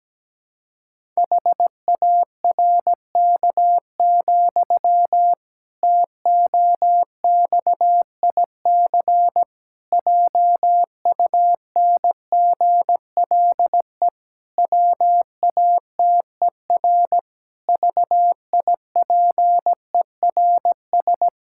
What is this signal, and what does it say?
Morse code, heard as a keyed tone, about 17 wpm, 700 Hz. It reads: HARK, TOXIC JUNGLE WATER VIPERS
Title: Pangram Challenge v2 - 17wpm